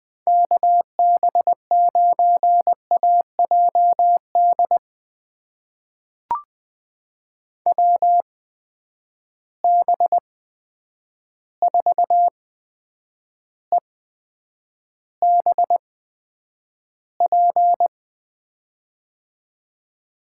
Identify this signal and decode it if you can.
Morse code, heard as a keyed tone, about 20 wpm, spaced as for 9 wpm, 700 Hz. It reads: KB9AJD WB4EBP
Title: USA Call Signs - 8x Character Spacing - 20wpm